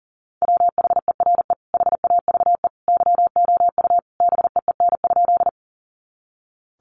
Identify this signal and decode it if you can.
Morse code, heard as a keyed tone, about 37 wpm, 700 Hz. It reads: WHERE HAVE YOU BEEN?
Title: Sentences from Top 200 Words 40wpm